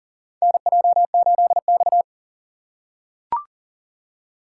Morse code, heard as a keyed tone, about 40 wpm, 700 Hz. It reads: NJ8X
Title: USA Call Signs - 4 Characters - 4x Character Spacing - 40wpm